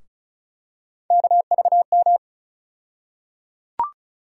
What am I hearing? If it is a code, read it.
Morse code, heard as a keyed tone, about 35 wpm, 700 Hz. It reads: KVM
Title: USA Call Sign Postfixes - 3 Letters 35wpm